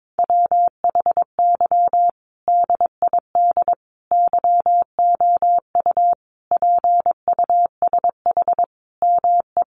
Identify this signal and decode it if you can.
Morse code, heard as a keyed tone, about 22 wpm, 700 Hz. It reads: WHY DID YOU PUSH ME
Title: Sentences from Top 1300 Words 22wpm